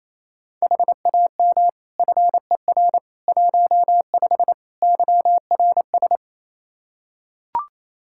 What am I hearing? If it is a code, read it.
Morse code, heard as a keyed tone, about 27 wpm, 700 Hz. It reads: HAM FER 15 YRS
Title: K1USN SST QSO Practice 28wpm